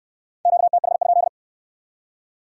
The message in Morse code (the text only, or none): QTH?